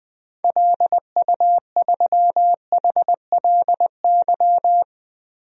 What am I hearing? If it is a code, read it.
Morse code, heard as a keyed tone, about 21 wpm, 700 Hz. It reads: LU3HLY
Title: DXCC-100 Call Signs - 6 Characters - Repeated 3x - 20wpm